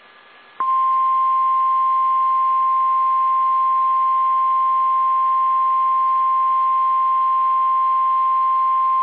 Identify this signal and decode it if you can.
Emergency Alert System (SAME) tones (the 1050 Hz attention tone). It is heard from 0.7 to 8.7 s.